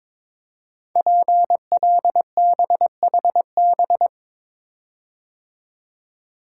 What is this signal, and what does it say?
Morse code, heard as a keyed tone, about 22 wpm, 700 Hz. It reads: PLBHB
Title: Code Groups - 5 Letter:Numbers - 22wpm